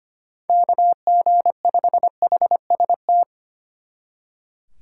Morse code, heard as a keyed tone, about 32 wpm, 700 Hz. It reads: KG5HST